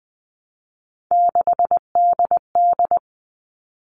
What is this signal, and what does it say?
Morse code, heard as a keyed tone, about 19 wpm, 700 Hz. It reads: 6DD
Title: All Cut Numbers - 3-Digit 20wpm